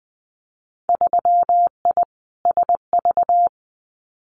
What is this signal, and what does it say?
Morse code, heard as a keyed tone, about 20 wpm, 700 Hz. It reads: 3I SV
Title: Winter Field Day Exchanges Encore 20wpm